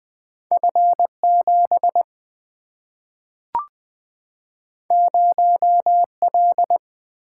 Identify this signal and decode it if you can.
Morse code, heard as a keyed tone, about 20 wpm, 700 Hz. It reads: F7 0L
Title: Single-Letter Single-Number 20wpm